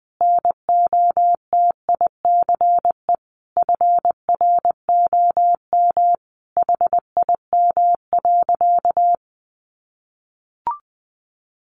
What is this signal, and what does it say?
Morse code, heard as a keyed tone, about 20 wpm, 700 Hz. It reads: NOTICE FROM HIM.
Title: Sentences from Top 400 Words 20wpm